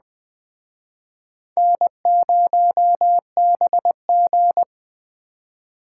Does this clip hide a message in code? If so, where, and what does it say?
Yes: Morse code from 1.6 to 4.6 s, heard as a keyed tone, about 20 wpm, 700 Hz. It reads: N0BG